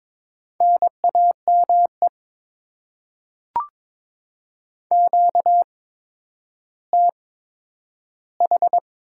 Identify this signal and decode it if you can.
Morse code, heard as a keyed tone, about 22 wpm, spaced as for 11 wpm, 700 Hz. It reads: NAME QTH
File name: Top 20 QSO Elements - 8x Character Spacing - 22wpm